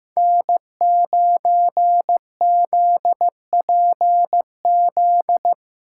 Morse code, heard as a keyed tone, about 18 wpm, 700 Hz. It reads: N9ZPZ
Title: Speed Racing - USA 1x3 Call Signs 15wpm